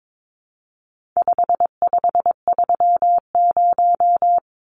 Morse code, heard as a keyed tone, about 22 wpm, 700 Hz. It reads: 5530
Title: Numbers - 4-Digit 22wpm